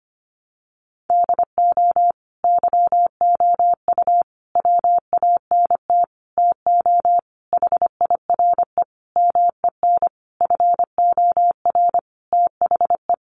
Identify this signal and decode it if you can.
Morse code, heard as a keyed tone, about 25 wpm, 700 Hz. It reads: DO YOU WANT TO HIRE MEN FOR THE
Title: Sentences from Top 1400 Words 25wpm